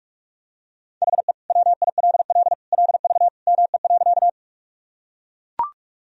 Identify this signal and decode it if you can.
Morse code, heard as a keyed tone, about 44 wpm, 700 Hz. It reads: HE WILL LV ME.
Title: Sentences from Top 400 Words and Abbreviations 45wpm